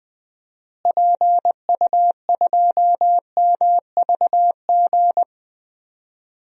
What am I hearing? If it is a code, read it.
Morse code, heard as a keyed tone, about 21 wpm, 700 Hz. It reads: PU2MVG
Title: DXCC-100 Call Signs - 6 Characters - Repeated 2x - 20wpm